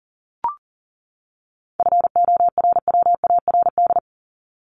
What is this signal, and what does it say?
Morse code, heard as a keyed tone, about 39 wpm, 700 Hz. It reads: FORWARD
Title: Top 1251-1500 Words 40wpm